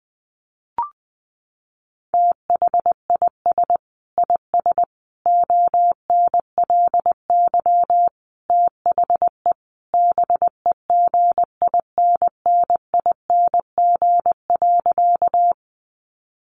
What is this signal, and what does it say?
Morse code, heard as a keyed tone, about 20 wpm, 700 Hz. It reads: THIS IS ONLY THE BEGINNING.
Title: Sentences from Top 800 Words 20wpm